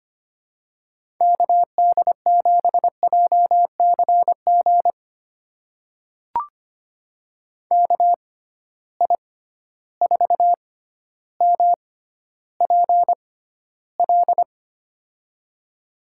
Morse code, heard as a keyed tone, about 25 wpm, spaced as for 13 wpm, 700 Hz. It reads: KD7JCG KI4MPL
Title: USA Call Signs - 6x Character Spacing - 25wpm